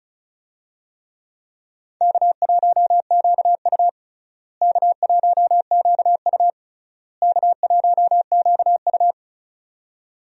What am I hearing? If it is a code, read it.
Morse code, heard as a keyed tone, about 36 wpm, 700 Hz. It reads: K1QU K1QU K1QU
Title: USA Call Signs - 4 Characters - Repeated 3x - 35wpm